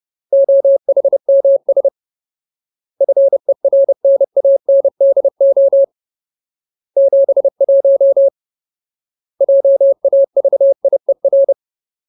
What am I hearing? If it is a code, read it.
Morse code, heard as a keyed tone, about 30 wpm, 550 Hz. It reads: OHMS FERNANDO 71 JAVIER
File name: Ultimate Word List / 30WPM Ultimate